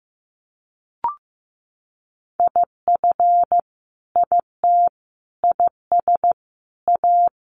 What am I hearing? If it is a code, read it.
Morse code, heard as a keyed tone, about 15 wpm, 700 Hz. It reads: IF IT IS A
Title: Sentences from Top 1500 Words 15wpm